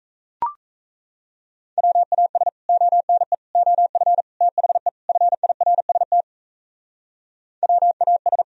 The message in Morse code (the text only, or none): WAS ONE OF THE FIRST WAS